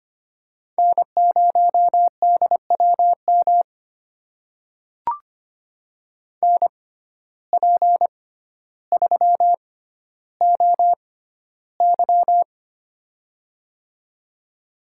Morse code, heard as a keyed tone, about 25 wpm, spaced as for 13 wpm, 700 Hz. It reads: N0DWM NP3OY
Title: USA Call Signs - 5 Characters - 6x Character Spacing - 25wpm